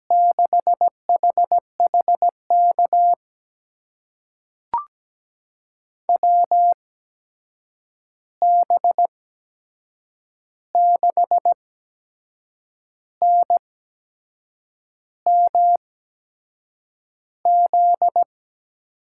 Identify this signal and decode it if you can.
Morse code, heard as a keyed tone, about 17 wpm, spaced as for 6 wpm, 700 Hz. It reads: 6HHK WB6NMZ
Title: USA Call Signs - 6 Characters - 8x Character Spacing - 17wpm